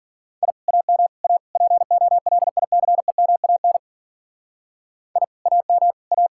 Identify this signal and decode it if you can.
Morse code, heard as a keyed tone, about 52 wpm, 700 Hz. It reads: I AM A POLICEMAN I AM A